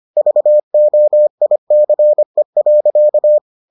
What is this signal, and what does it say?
Morse code, heard as a keyed tone, about 25 wpm, 600 Hz. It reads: VOICE.